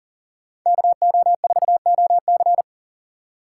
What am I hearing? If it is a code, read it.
Morse code, heard as a keyed tone, about 40 wpm, 700 Hz. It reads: KO4OC
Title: USA Call Signs - 5 Characters - 6x Character Spacing - 40wpm